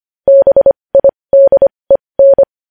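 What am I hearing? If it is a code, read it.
Morse code, heard as a keyed tone, about 25 wpm, spaced as for 20 wpm, 550 Hz. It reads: BIDEN
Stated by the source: Morse code news headlines sent at 20 wpm.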